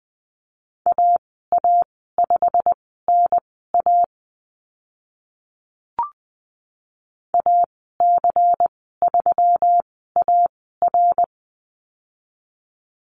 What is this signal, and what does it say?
Morse code, heard as a keyed tone, about 20 wpm, spaced as for 14 wpm, 700 Hz. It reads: AA5NA AC3AR
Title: 128 Lesson 21g - Number 6 - Call Signs 20wpm